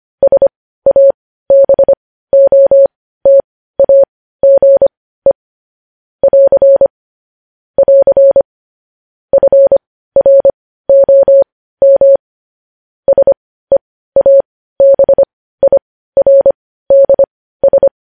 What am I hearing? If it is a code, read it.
Morse code, heard as a keyed tone, about 25 wpm, spaced as for 15 wpm, 550 Hz. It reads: SABOTAGE <AR> <AR> FROM SEABIRDS
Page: News Headlines in Morse Code at 15 WPM